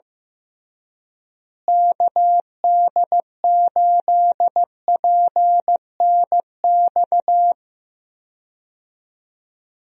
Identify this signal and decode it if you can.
Morse code, heard as a keyed tone, about 15 wpm, 700 Hz. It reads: KD8PNX
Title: USA Call Signs - 6 Characters - Repeated 3x - 15wpm